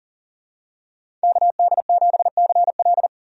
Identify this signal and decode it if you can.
Morse code, heard as a keyed tone, about 40 wpm, 700 Hz. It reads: KD7CL
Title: USA Call Signs - 5 Characters - 40wpm